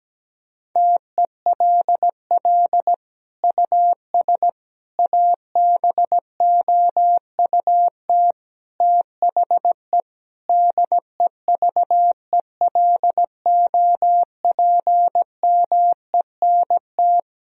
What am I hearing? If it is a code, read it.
Morse code, heard as a keyed tone, about 17 wpm, 700 Hz. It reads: TELL US ABOUT THE DEVELOPMENT
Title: Sentences from Top 1000 Words - Review Part 4 - 17wpm